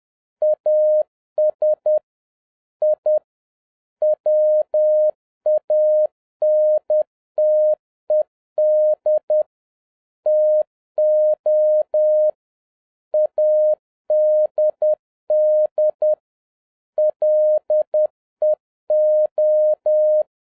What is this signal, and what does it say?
Morse code, heard as a keyed tone, about 10 wpm, 610 Hz. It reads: AS I WANTED TO ADD LEO